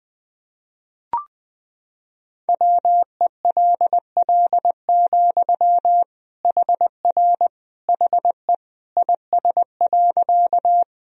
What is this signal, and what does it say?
Morse code, heard as a keyed tone, about 20 wpm, 700 Hz. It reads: WELL, HR HE IS.
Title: Sentences from Top 200 Words and Abbreviations 20wpm